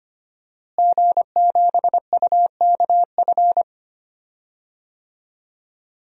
Morse code, heard as a keyed tone, about 25 wpm, 700 Hz. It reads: G7UKF
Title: DXCC-100 Call Signs - 5 Characters - Repeated 2x - 25wpm